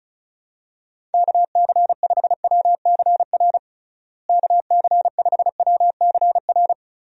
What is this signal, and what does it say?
Morse code, heard as a keyed tone, about 34 wpm, 700 Hz. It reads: KC5WCR KC5WCR
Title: USA Call Signs - 6 Characters - Repeated 2x - 35wpm